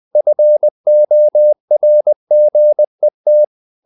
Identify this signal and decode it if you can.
Morse code, heard as a keyed tone, about 20 wpm, 600 Hz. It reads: FORGET
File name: the old man and the sea part 2 20wpm